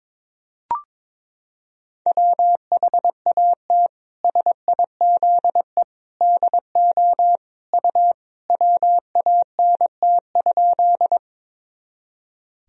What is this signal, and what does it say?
Morse code, heard as a keyed tone, about 22 wpm, 700 Hz. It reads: WHAT SIZE DO U WANT?
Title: Sentences from Top 400 Words and Abbreviations 22wpm